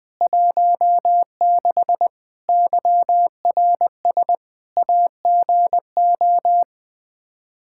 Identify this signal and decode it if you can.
Morse code, heard as a keyed tone, about 20 wpm, 700 Hz. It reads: 16 YRS AGO
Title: Practice QSO - Exchange 3 - More Info - 20wpm